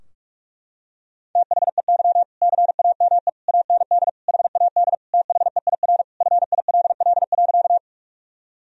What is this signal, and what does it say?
Morse code, heard as a keyed tone, about 46 wpm, 700 Hz. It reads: THEY CAME AND HAD THEIR FILL.
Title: Sentences from Top 1000 Words - Review Part 3 - 45wpm